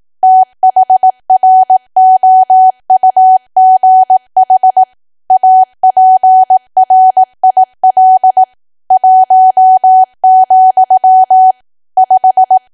Morse code, heard as a keyed tone, about 18 wpm, 750 Hz. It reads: THROUGH APRIL 1, 5
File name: cwa / W1AW-Level2 / 18WPM